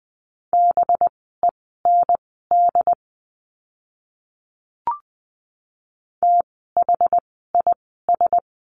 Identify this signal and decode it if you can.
Morse code, heard as a keyed tone, about 20 wpm, spaced as for 15 wpm, 700 Hz. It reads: BEND THIS
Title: 185 Lesson 30f - Letter b - Words 20wpm